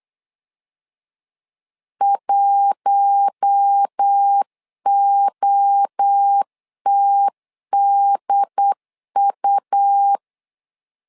Morse code, heard as a keyed tone, about 8 wpm, 800 Hz. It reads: 1OTDU
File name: LEZIONI / 4 Q1S 8wpm 2710